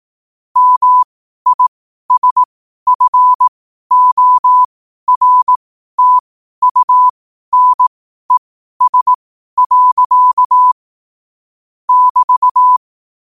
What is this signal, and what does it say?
Morse code, heard as a keyed tone, about 18 wpm, spaced as for 12 wpm, 1000 Hz. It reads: MISFORTUNES. <BT>
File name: qotdfiles / 12wpm